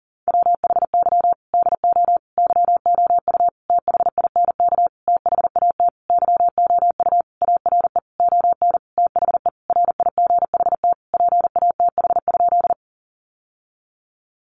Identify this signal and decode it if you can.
Morse code, heard as a keyed tone, about 40 wpm, 700 Hz. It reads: WHY DO YOU THINK THAT YOU ARE ON THE RIGHT PATH?
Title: Sentences from Top 1400 Words 40wpm